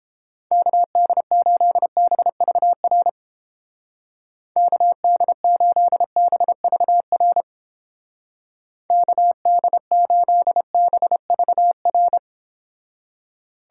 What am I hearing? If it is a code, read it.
Morse code, heard as a keyed tone, about 30 wpm, 700 Hz. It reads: KD8BVR KD8BVR KD8BVR